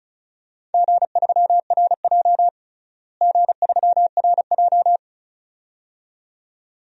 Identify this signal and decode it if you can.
Morse code, heard as a keyed tone, about 36 wpm, 700 Hz. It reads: G3RJ G3RJ
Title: DXCC-100 Call Signs - 4 Characters - Repeated 2x - 35wpm